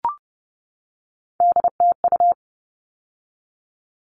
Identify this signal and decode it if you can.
Morse code, heard as a keyed tone, about 30 wpm, 700 Hz. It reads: DTU
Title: All Cut Numbers - 3-Digit 30wpm